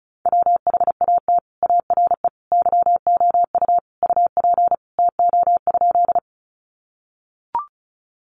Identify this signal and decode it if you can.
Morse code, heard as a keyed tone, about 34 wpm, 700 Hz. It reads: WHAT ARE YOU UP TO?
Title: Sentences from Top 100 Words 35wpm